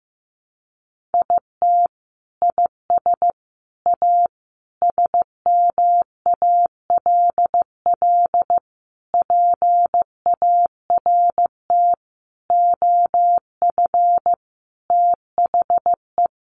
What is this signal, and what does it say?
Morse code, heard as a keyed tone, about 14 wpm, 700 Hz. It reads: IT IS A SMALL PART OF THE
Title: Sentences from Top 1500 Words 15wpm